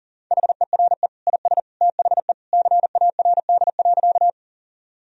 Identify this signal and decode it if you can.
Morse code, heard as a keyed tone, about 40 wpm, 700 Hz. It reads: HERE IS THE CARD.
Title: Sentences from Top 1000 Words - Review Part 4 - 40wpm